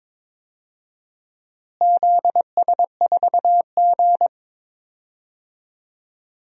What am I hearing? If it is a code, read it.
Morse code, heard as a keyed tone, about 22 wpm, 700 Hz. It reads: ZS4G
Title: DXCC-100 Call Signs - 4 Characters - Repeated 2x - 22wpm